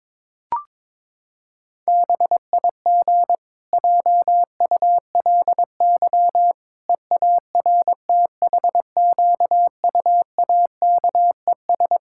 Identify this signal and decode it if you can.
Morse code, heard as a keyed tone, about 22 wpm, 700 Hz. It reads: BIG JULY EARTHQUAKES
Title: Pangram Challenge v2 - 22wpm